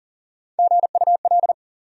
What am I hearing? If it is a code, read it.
Morse code, heard as a keyed tone, about 38 wpm, 700 Hz. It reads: GUL
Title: USA Call Sign Postfixes - 3 Letters 40wpm